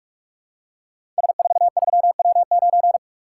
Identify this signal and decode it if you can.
Morse code, heard as a keyed tone, about 44 wpm, 700 Hz. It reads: S43W9
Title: Code Groups - 5 Letter:Numbers - 45wpm